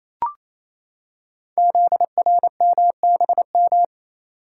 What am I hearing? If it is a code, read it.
Morse code, heard as a keyed tone, about 29 wpm, 700 Hz. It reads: ZRMBM
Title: Code Groups - 5 Letters - 28wpm